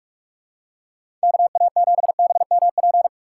Morse code, heard as a keyed tone, about 46 wpm, 700 Hz. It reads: KA7BMP
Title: USA Call Signs - 6 Characters - 2x Character Spacing - 45wpm